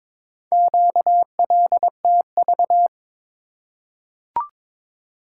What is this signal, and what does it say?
Morse code, heard as a keyed tone, about 22 wpm, 700 Hz. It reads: QLTV
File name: practice / 22wpm / Four Letter-Number Combinations - 22wpm